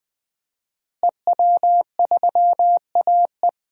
Morse code, heard as a keyed tone, about 20 wpm, 700 Hz. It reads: EW3AE
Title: DXCC-100 Call Signs - 5 Characters - Repeated 2x - 20wpm